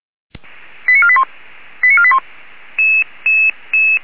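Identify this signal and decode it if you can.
Five-tone selective call (ZVEI), heard as quick runs of five tones, tones of about 70 ms. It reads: 89481 89481
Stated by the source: ZVEI five-tone selective call received by SDR.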